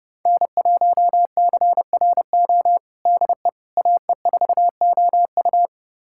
Morse code, heard as a keyed tone, about 32 wpm, 700 Hz. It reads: N1CRO DE AE4OU
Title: USA Callsign DE Callsign - 5 Characters - 30wpm